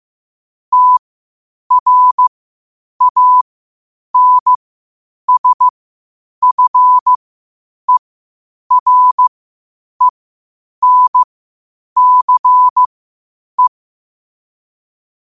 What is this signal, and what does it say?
Morse code, heard as a keyed tone, about 15 wpm, spaced as for 8 wpm, 1000 Hz. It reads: TRANSFERENCE
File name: qotdfiles / 8wpm